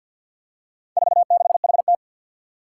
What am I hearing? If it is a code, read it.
Morse code, heard as a keyed tone, about 48 wpm, 700 Hz. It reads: 46HT
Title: Four Challenging Letter-Number Combinations 50wpm